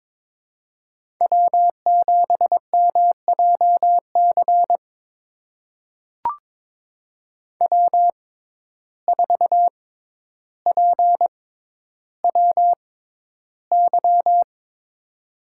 Morse code, heard as a keyed tone, about 22 wpm, spaced as for 11 wpm, 700 Hz. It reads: W7MJC W4PWY
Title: USA Call Signs - 5 Characters - 6x Character Spacing - 22wpm